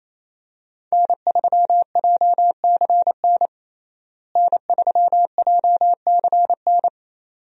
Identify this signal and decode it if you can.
Morse code, heard as a keyed tone, about 28 wpm, 700 Hz. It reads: N3JCN N3JCN
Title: USA Call Signs - 5 Characters - Repeated 2x - 28wpm